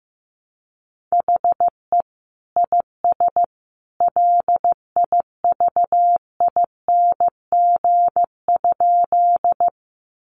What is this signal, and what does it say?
Morse code, heard as a keyed tone, about 15 wpm, 700 Hz. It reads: HE IS LIVING?
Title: Sentences from Top 1100 Words 15wpm